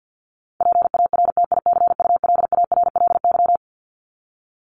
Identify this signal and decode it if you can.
Morse code, heard as a keyed tone, about 49 wpm, 700 Hz. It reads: PARTICULARLY
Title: LY Suffix - Top 100 Words with Base Words 50wpm